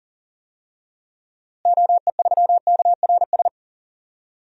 Morse code, heard as a keyed tone, about 40 wpm, 700 Hz. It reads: OE3KRS
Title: DXCC-100 Call Signs - 6 Characters - Repeated 2x - 40wpm